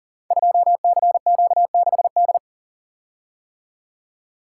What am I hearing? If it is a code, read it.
Morse code, heard as a keyed tone, about 42 wpm, 690 Hz. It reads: 2CQ6D
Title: Code Groups - 5 Letter:Numbers - 40wpm